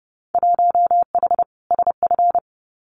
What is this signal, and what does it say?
Morse code, heard as a keyed tone, about 30 wpm, 700 Hz. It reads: 1H SF
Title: Winter Field Day Exchanges 30wpm